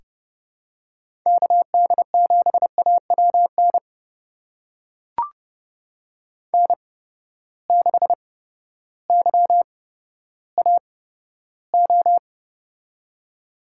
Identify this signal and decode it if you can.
Morse code, heard as a keyed tone, about 31 wpm, spaced as for 14 wpm, 700 Hz. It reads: KD7AWN N6YAO
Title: USA Call Signs - 8x Character Spacing - 30wpm